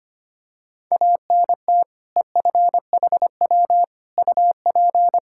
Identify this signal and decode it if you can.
Morse code, heard as a keyed tone, about 25 wpm, 700 Hz. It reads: ANT EFHW UP
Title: Practice QSO - Exchange 2 - Info - 25wpm